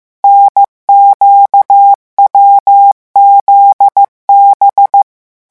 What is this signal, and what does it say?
Morse code, heard as a keyed tone, about 15 wpm, 800 Hz. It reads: NQWZB
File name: morse mp3 / revision8-vitesse 14 mots